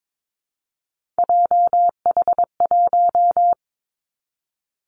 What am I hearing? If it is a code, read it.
Morse code, heard as a keyed tone, about 22 wpm, 700 Hz. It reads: JH1
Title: Callsign Prefixes for Top 25 DXCC - 22wpm